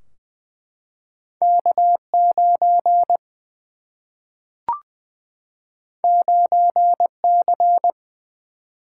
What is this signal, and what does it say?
Morse code, heard as a keyed tone, about 20 wpm, 700 Hz. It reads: K 9 9 C
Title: Single-Letter Single-Number 20wpm